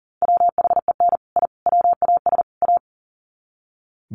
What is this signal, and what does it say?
Morse code, heard as a keyed tone, about 40 wpm, 700 Hz. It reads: WHEN I WAS A
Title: Sets of 4 Words 40wpm